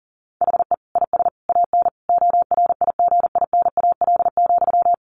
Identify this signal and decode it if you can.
Morse code, heard as a keyed tone, about 40 wpm, 700 Hz. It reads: HE IS AN ORIGINAL,
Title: Sentences from Top 900 Words 40wpm